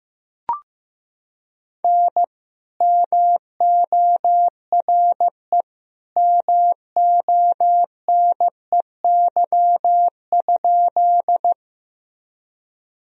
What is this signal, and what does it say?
Morse code, heard as a keyed tone, about 15 wpm, 700 Hz. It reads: N MORE MONEY?
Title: Sentences from Top 400 Words and Abbreviations 15wpm